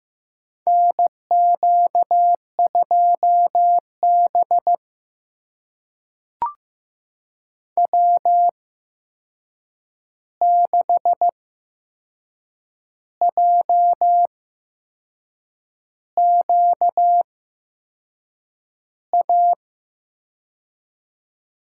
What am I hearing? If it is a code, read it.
Morse code, heard as a keyed tone, about 15 wpm, spaced as for 6 wpm, 700 Hz. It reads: NQ2B W6JQA
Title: USA Call Signs - 8x Character Spacing - 15wpm